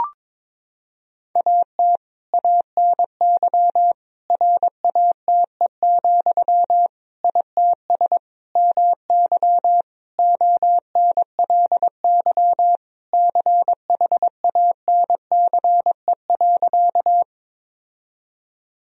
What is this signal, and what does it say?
Morse code, heard as a keyed tone, about 22 wpm, 700 Hz. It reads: AT ANY RATE, ITS MY ONLY CHANCE.
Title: Sentences from Top 1000 Words - Review Part 4 - 22wpm